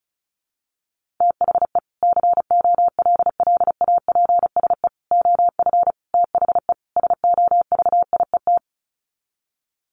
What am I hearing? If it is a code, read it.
Morse code, heard as a keyed tone, about 34 wpm, 700 Hz. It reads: THE COLLAPSE OF THE SOVIET
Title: Sets of 5 Words Encore 35wpm